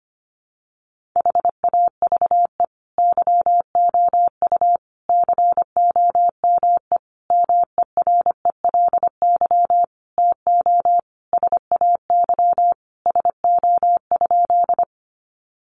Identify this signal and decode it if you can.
Morse code, heard as a keyed tone, about 25 wpm, 700 Hz. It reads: HAVE YOU COME MERELY TO SAY SO?
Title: Sentences from Top 1400 Words 25wpm